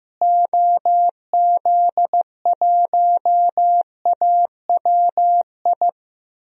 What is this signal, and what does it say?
Morse code, heard as a keyed tone, about 15 wpm, 700 Hz. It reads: OZ1AWI
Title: DXCC-100 Call Signs - 6 Characters - Repeated 3x - 15wpm